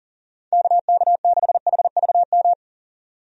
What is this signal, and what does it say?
Morse code, heard as a keyed tone, about 40 wpm, 700 Hz. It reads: KK6HVM
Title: USA Call Signs - 6 Characters - 40wpm